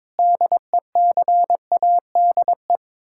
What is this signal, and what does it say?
Morse code, heard as a keyed tone, about 22 wpm, 700 Hz. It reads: DECADE
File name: practice / 22wpm / Sets of 7 Words - 22wpm